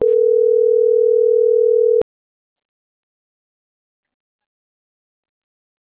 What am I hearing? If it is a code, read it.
Ringback tone (North America), 440 + 480 Hz.